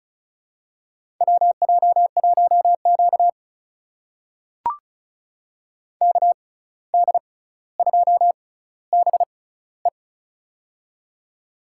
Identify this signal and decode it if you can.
Morse code, heard as a keyed tone, about 36 wpm, spaced as for 17 wpm, 700 Hz. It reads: WJ1Q KD2BE
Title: USA Call Signs - 6x Character Spacing - 35wpm